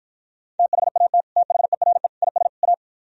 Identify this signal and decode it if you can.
Morse code, heard as a keyed tone, about 53 wpm, 700 Hz. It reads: THAT THERE IS A